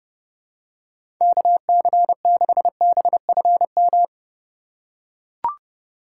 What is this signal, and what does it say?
Morse code, heard as a keyed tone, about 30 wpm, 700 Hz. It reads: KC6BFM
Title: USA Call Signs - 6 Characters - 2x Character Spacing - 30wpm